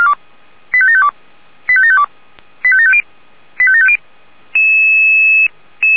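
Five-tone selective call (ZVEI), heard as quick runs of five tones, tones of about 70 ms. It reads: 85752 85752 85750 85750
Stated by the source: ZVEI five-tone selective call received on SDR, from the audio server of feuerwehr-orlamuende.org (fire brigade Orlamünde).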